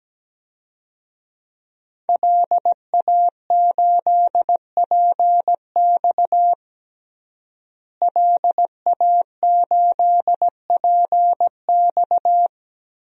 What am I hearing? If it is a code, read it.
Morse code, heard as a keyed tone, about 17 wpm, 700 Hz. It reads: LA8PX LA8PX
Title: DXCC-100 Call Signs - 5 Characters - Repeated 3x - 17wpm